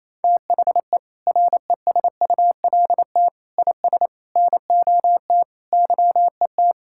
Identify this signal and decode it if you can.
Morse code, heard as a keyed tone, about 28 wpm, 700 Hz. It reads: THE RESULT IS NOT YET
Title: Sentences from Top 1000 Words - Review Part 5 - 28wpm